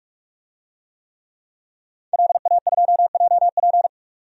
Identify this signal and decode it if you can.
Morse code, heard as a keyed tone, about 46 wpm, 700 Hz. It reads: LA2JP
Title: DXCC-100 Call Signs - 5 Characters - 45wpm